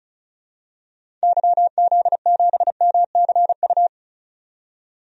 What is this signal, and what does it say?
Morse code, heard as a keyed tone, about 34 wpm, 700 Hz. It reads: YZ7MCU
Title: DXCC-100 Call Signs - 6 Characters - 35wpm